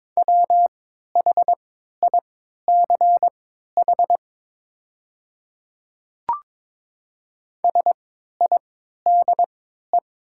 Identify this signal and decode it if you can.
Morse code, heard as a keyed tone, about 22 wpm, spaced as for 12 wpm, 700 Hz. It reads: WHICH SIDE
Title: 3x Character Spacing - Top 100 Words - 22wpm